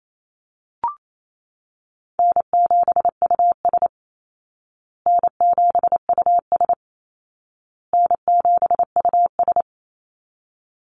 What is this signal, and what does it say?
Morse code, heard as a keyed tone, about 28 wpm, 700 Hz. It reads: N7US N7US N7US
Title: Call Signs Repeated 2x - 28wpm